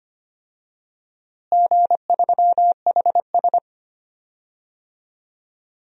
Morse code, heard as a keyed tone, about 25 wpm, 700 Hz. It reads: G3HS
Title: DXCC-100 Call Signs - 4 Characters - Repeated 3x - 25wpm